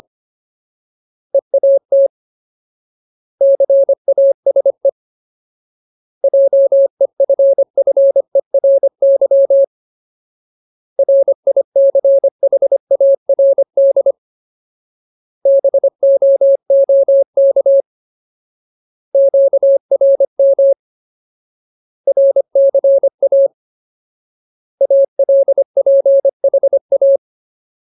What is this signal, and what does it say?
Morse code, heard as a keyed tone, about 25 wpm, 550 Hz. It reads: EAT CASE JEFFERY RICHARD BOOK QRM RCA ALPHA
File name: Ultimate Word List / 25WPM Ultimate